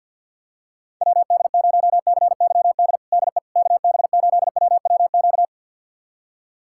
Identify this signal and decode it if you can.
Morse code, heard as a keyed tone, about 49 wpm, 700 Hz. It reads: WD0CYD DE KB8WWQ
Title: USA Callsign DE Callsign - 6 Characters - 50wpm